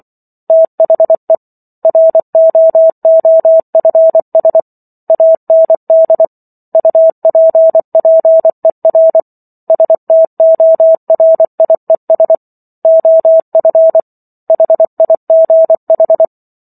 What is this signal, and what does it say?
Morse code, heard as a keyed tone, about 24 wpm, 650 Hz. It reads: THE ROOFS AND UPPER STORIES OF HIGH